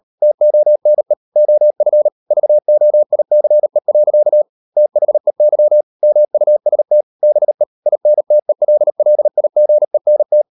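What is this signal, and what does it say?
Morse code, heard as a keyed tone, about 38 wpm, 600 Hz. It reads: TONE OF VOICE. THEY MUST BE INTELLIGENT
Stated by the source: Morse code practice at 38 WPM, sending Ken Follett's novel Never in English, from CW med Gnister.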